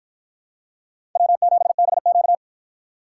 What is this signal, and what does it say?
Morse code, heard as a keyed tone, about 53 wpm, 700 Hz. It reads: W7BQ